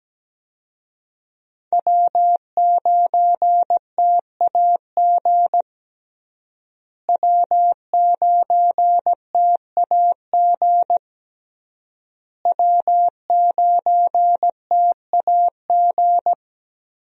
Morse code, heard as a keyed tone, about 17 wpm, 700 Hz. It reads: W9TAG W9TAG W9TAG
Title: USA Call Signs - 5 Characters - Repeated 3x - 17wpm